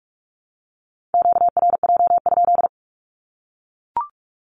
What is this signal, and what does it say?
Morse code, heard as a keyed tone, about 44 wpm, 700 Hz. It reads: QRJ?
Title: Q-Codes 45wpm